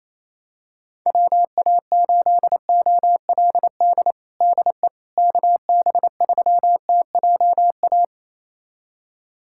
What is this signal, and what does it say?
Morse code, heard as a keyed tone, about 28 wpm, 700 Hz. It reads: WA8OLD DE KB3TJA
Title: USA Callsign DE Callsign - 6 Characters - 28wpm